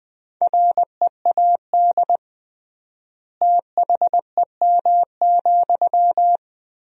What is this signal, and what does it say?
Morse code, heard as a keyed tone, about 20 wpm, 700 Hz. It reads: READ THEM,
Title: 3x Word-Spacing - Sentences from Top 400 Words - 20wpm